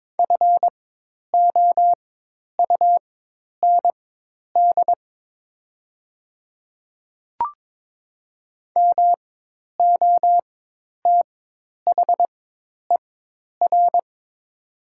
Morse code, heard as a keyed tone, about 22 wpm, spaced as for 10 wpm, 700 Hz. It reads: FOUND MOTHER
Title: 4x Character Spacing - Top 300 Words - 22wpm